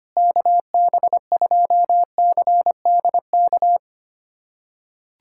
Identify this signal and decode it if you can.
Morse code, heard as a keyed tone, about 25 wpm, 700 Hz. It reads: KB2CDK
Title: USA Callsign DE Callsign 25wpm